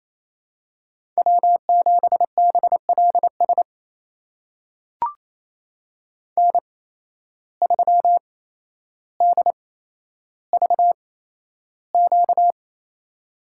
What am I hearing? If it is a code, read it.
Morse code, heard as a keyed tone, about 28 wpm, spaced as for 12 wpm, 700 Hz. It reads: W7BLS N3DVQ
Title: USA Call Signs - 5 Characters - 8x Character Spacing - 28wpm